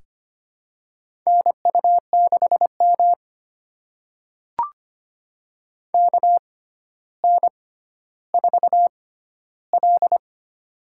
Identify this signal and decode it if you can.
Morse code, heard as a keyed tone, about 25 wpm, spaced as for 13 wpm, 700 Hz. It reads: NU6M KN4L
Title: USA Call Signs - 4 Characters - 6x Character Spacing - 25wpm